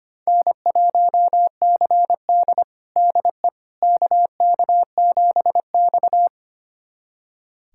Morse code, heard as a keyed tone, about 25 wpm, 700 Hz. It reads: N1CD DE KK7X
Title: USA Callsign DE Callsign - 4 Characters - 25wpm